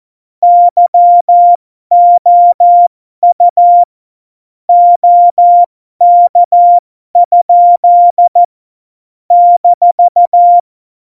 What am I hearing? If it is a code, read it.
Morse code, heard as a keyed tone, about 14 wpm, spaced as for 12 wpm, 700 Hz. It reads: YOU OK? -
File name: Morsekurs / Witze